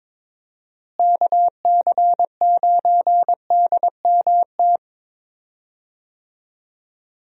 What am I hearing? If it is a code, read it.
Morse code, heard as a keyed tone, about 21 wpm, 700 Hz. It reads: KC9DMT
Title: USA Call Signs - 6 Characters - Repeated 2x - 22wpm